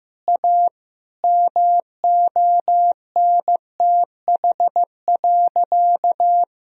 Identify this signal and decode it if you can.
Morse code, heard as a keyed tone, about 15 wpm, 700 Hz. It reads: A MONTH.
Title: Sentences from Top 1000 Words - Review Part 3 - 15wpm